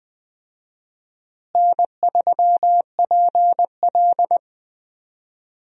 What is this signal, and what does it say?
Morse code, heard as a keyed tone, about 20 wpm, 700 Hz. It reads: N3PL
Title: USA Call Signs - 4 Characters - Repeated 3x - 20wpm